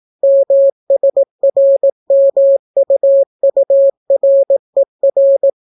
Morse code, heard as a keyed tone, about 18 wpm, 550 Hz. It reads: MSRMUURER